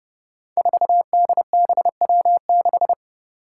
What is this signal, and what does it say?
Morse code, heard as a keyed tone, about 30 wpm, 700 Hz. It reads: 4DBW6
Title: Five Challenging Letter-Number Combinations 30wpm